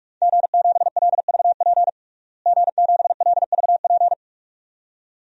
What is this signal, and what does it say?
Morse code, heard as a keyed tone, about 44 wpm, 700 Hz. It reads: G7LVP G7LVP
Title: DXCC-100 Call Signs - 5 Characters - Repeated 2x - 45wpm